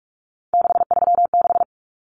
Morse code, heard as a keyed tone, about 44 wpm, 700 Hz. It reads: 636
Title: Numbers - 3-Digit 45wpm